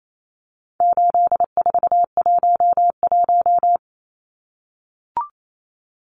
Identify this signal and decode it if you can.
Morse code, heard as a keyed tone, about 28 wpm, 700 Hz. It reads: 8411
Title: Numbers - 4-Digit 28wpm